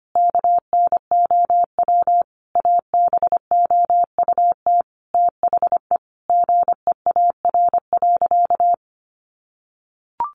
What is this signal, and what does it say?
Morse code, heard as a keyed tone, about 25 wpm, 700 Hz. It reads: KNOW ABOUT THE GEAR.
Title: Sentences from Top 1400 Words 25wpm